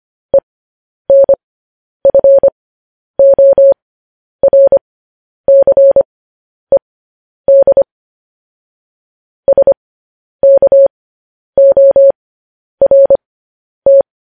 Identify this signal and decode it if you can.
Morse code, heard as a keyed tone, about 25 wpm, spaced as for 10 wpm, 550 Hz. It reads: ENFORCED SKORT
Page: news-headlines-in-morse-code-at-10-wpm